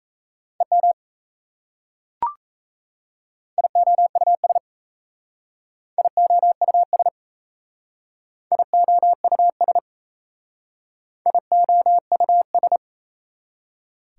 Morse code, heard as a keyed tone, about 37 wpm, 690 Hz. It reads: EM IOUS IOUS IOUS IOUS